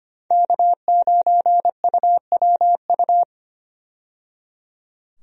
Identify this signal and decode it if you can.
Morse code, heard as a keyed tone, about 25 wpm, 700 Hz. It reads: K9UWU
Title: USA Call Signs - 5 Characters - 25wpm